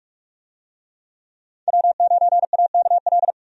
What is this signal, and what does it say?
Morse code, heard as a keyed tone, about 45 wpm, 700 Hz. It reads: W9AKL
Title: USA Call Signs - 5 Characters - 45wpm